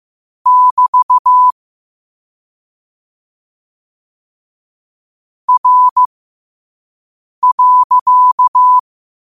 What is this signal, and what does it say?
Morse code, heard as a keyed tone, about 15 wpm, spaced as for 5 wpm, 1000 Hz. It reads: <BT> R.
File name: qotdfiles / 5wpm